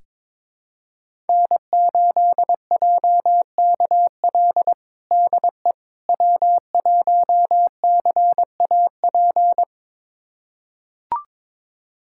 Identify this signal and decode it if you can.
Morse code, heard as a keyed tone, about 22 wpm, 700 Hz. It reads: N8JKL DE W1CAP
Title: USA Callsign DE Callsign - 5 Characters - 22wpm